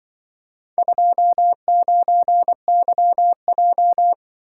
Morse code, heard as a keyed tone, about 23 wpm, 700 Hz. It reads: 29YJ
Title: Speed-Racing - Four Letter-Number Combinations - 20wpm